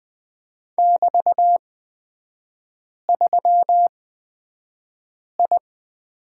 Morse code, heard as a keyed tone, about 20 wpm, spaced as for 5 wpm, 700 Hz. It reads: <BT>3I
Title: 250 Lesson 40e - BK - Three Characters 20wpm